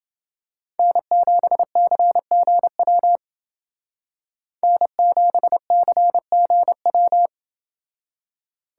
Morse code, heard as a keyed tone, about 28 wpm, 700 Hz. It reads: N7CGW N7CGW